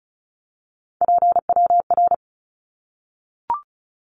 Morse code, heard as a keyed tone, about 34 wpm, 700 Hz. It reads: PWR
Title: Abbreviations Repeated 3x - 35wpm